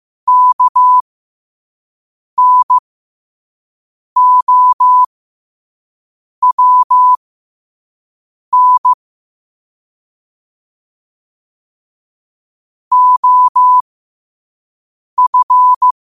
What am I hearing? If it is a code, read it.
Morse code, heard as a keyed tone, about 15 wpm, spaced as for 5 wpm, 1000 Hz. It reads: KNOWN OF
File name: qotdfiles / 5wpm